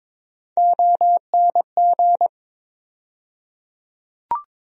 Morse code, heard as a keyed tone, about 22 wpm, 700 Hz. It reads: ONG
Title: USA Call Sign Postfixes - 3 Letters 22wpm